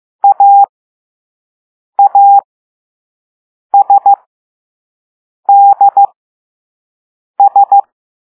Morse code, heard as a keyed tone, about 15 wpm, spaced as for 5 wpm, 800 Hz. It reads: AASDS